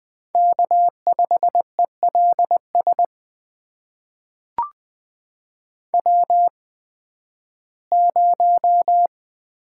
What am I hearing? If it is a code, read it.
Morse code, heard as a keyed tone, about 20 wpm, spaced as for 13 wpm, 700 Hz. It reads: K5ELS W0
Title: USA Call Signs - 5 Characters - 8x Character Spacing - 20wpm